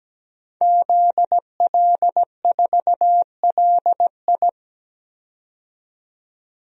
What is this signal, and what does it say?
Morse code, heard as a keyed tone, about 17 wpm, 700 Hz. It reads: ZL4LI
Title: DXCC-100 Call Signs - 5 Characters - 17wpm